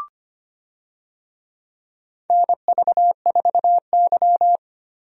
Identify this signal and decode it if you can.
Morse code, heard as a keyed tone, about 25 wpm, 700 Hz. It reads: NV4Y
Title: USA Call Signs - 4 Characters - Repeated 3x - 25wpm